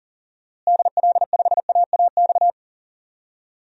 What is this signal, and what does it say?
Morse code, heard as a keyed tone, about 37 wpm, 700 Hz. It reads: DL5AAX
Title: DXCC-100 Call Signs - 6 Characters - Repeated 2x - 40wpm